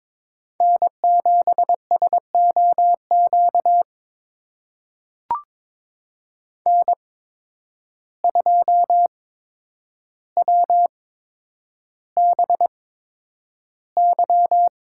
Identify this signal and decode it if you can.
Morse code, heard as a keyed tone, about 22 wpm, spaced as for 9 wpm, 700 Hz. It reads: N7SOQ N2WBY
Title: USA Call Signs - 5 Characters - 8x Character Spacing - 22wpm